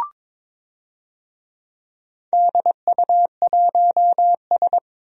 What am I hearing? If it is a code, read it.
Morse code, heard as a keyed tone, about 22 wpm, 700 Hz. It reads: DU1S
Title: DXCC-100 Call Signs - 4 Characters - 22wpm